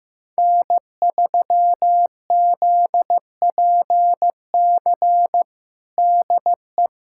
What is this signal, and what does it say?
Morse code, heard as a keyed tone, about 15 wpm, 700 Hz. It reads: N3ZPC DE
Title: USA Callsign DE Callsign - 5 Characters - 15wpm